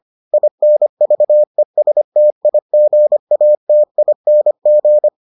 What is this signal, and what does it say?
Morse code, heard as a keyed tone, about 25 wpm, 600 Hz. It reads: INVESTIGATING